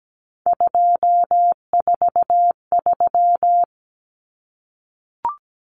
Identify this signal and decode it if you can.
Morse code, heard as a keyed tone, about 17 wpm, 700 Hz. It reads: 243
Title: Common Cut Numbers - 3-Digit 17wpm